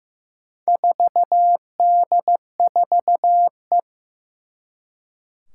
Morse code, heard as a keyed tone, about 15 wpm, 700 Hz. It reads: 4D4E
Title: Four Challenging Letter-Number Combinations 15wpm